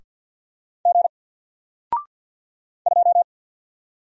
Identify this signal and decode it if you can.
Morse code, heard as a keyed tone, about 49 wpm, 700 Hz. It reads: G 2
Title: Single Letter-Number Alt 50wpm